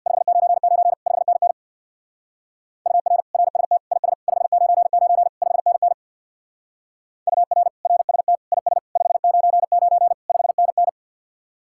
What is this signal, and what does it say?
Morse code, heard as a keyed tone, about 59 wpm, 700 Hz. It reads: 599 5NN UR RST IS 599 5NN UR RST IS 599 5NN
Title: Speed-Racing Common QSO Phrases - 50wpm